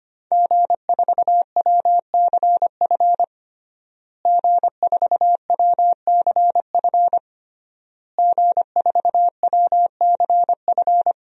Morse code, heard as a keyed tone, about 25 wpm, 700 Hz. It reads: G4WCF G4WCF G4WCF
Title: DXCC-100 Call Signs - 5 Characters - Repeated 3x - 25wpm